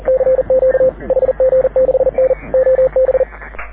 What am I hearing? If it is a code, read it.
Morse code, heard as a keyed tone, about 40 wpm, 520 Hz. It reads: CQ HG6N OB
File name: AUDIO2019 / HG6N CW SSB MIX